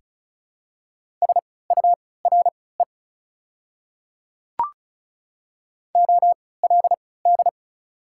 Morse code, heard as a keyed tone, about 34 wpm, spaced as for 20 wpm, 700 Hz. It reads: SURE OLD
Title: 3x Character Spacing - Top 300 Words - 35wpm